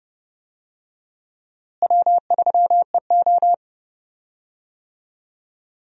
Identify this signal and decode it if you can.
Morse code, heard as a keyed tone, about 30 wpm, 700 Hz. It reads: W3EO
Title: USA Call Signs - 4 Characters - Repeated 3x - 30wpm